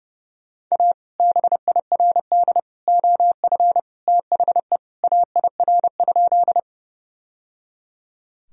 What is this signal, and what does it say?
Morse code, heard as a keyed tone, about 30 wpm, 700 Hz. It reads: A BIRD OF THE AIR?
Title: Sentences from Top 1000 Words - Review Part 2 - 30wpm